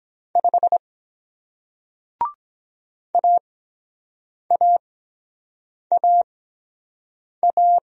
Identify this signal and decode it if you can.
Morse code, heard as a keyed tone, about 23 wpm, spaced as for 7 wpm, 700 Hz. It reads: 5 AAAA